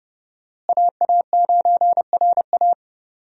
Morse code, heard as a keyed tone, about 30 wpm, 700 Hz. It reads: AA9RA
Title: USA Call Signs - 5 Characters - Repeated 2x - 30wpm